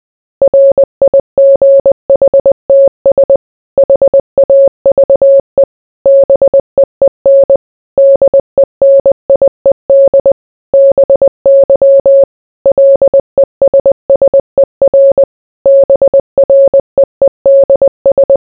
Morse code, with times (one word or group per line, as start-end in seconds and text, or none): 0.4-3.4 RIGHTS
3.8-5.6 HAVE
6.1-7.6 BEEN
8.0-10.3 DENIED
10.7-12.2 BY
12.7-15.2 LESSER
15.7-18.4 BREEDS